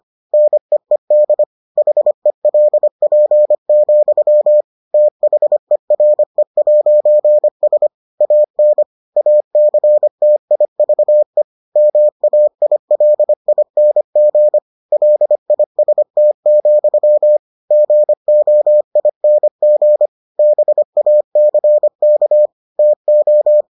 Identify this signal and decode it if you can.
Morse code, heard as a keyed tone, about 25 wpm, 600 Hz. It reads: NEED HELP, THERE'S AN ACTIVE MAILING LIST, GOING BACK TO